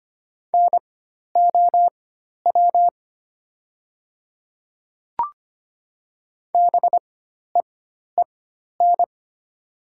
Morse code, heard as a keyed tone, about 25 wpm, spaced as for 12 wpm, 700 Hz. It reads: NOW BEEN
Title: 4x Character Spacing - Top 100 Words - 25wpm